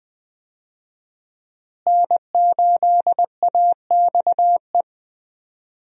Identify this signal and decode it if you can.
Morse code, heard as a keyed tone, about 21 wpm, 700 Hz. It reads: N8AXE
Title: USA Call Signs - 5 Characters - Repeated 3x - 20wpm